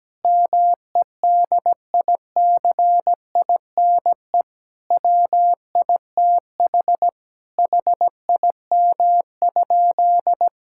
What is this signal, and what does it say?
Morse code, heard as a keyed tone, about 17 wpm, 700 Hz. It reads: MEDICINE WITH HIM?
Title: Sentences from Top 1000 Words - Review Part 6 - 17wpm